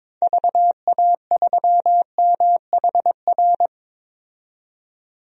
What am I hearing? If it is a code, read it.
Morse code, heard as a keyed tone, about 22 wpm, 700 Hz. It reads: VA3MHR
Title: DXCC-100 Call Signs - 6 Characters - 22wpm